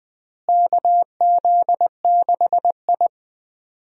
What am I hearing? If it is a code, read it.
Morse code, heard as a keyed tone, about 20 wpm, 700 Hz. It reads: KZ6I
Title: USA Call Signs - 4 Characters - Repeated 3x - 20wpm